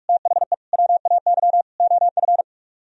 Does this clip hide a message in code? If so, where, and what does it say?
Yes: Morse code from 0.1 to 2.4 s, heard as a keyed tone, about 47 wpm, 700 Hz. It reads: THE WAY OF